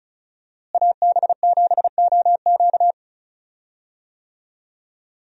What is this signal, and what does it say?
Morse code, heard as a keyed tone, about 34 wpm, 700 Hz. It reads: AB7OQ
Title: USA Call Signs - 5 Characters - Repeated 3x - 35wpm